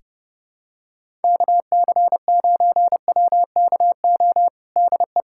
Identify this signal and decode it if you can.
Morse code, heard as a keyed tone, about 30 wpm, 700 Hz. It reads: KC9WKO DE
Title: USA Callsign DE Callsign 30wpm